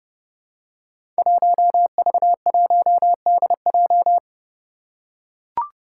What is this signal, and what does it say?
Morse code, heard as a keyed tone, about 29 wpm, 700 Hz. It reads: 1V1DJ
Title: Five Challenging Letter-Number Combinations 30wpm